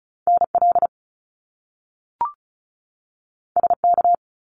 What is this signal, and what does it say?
Morse code, heard as a keyed tone, about 36 wpm, 700 Hz. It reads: NL SK
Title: Canadian Provinces and Territories 35wpm